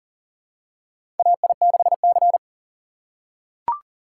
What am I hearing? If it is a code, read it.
Morse code, heard as a keyed tone, about 40 wpm, 700 Hz. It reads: AI6C
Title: USA Call Signs - 4 Characters - 3x Character Spacing - 40wpm